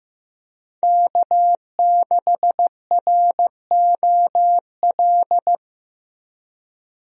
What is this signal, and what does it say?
Morse code, heard as a keyed tone, about 15 wpm, 700 Hz. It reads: K6ROL
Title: USA Call Signs - 5 Characters - Repeated 2x - 15wpm